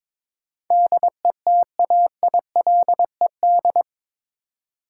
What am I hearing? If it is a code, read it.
Morse code, heard as a keyed tone, about 22 wpm, 700 Hz. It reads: DETAILED
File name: practice / 22wpm / Eight Letter Words - 22wpm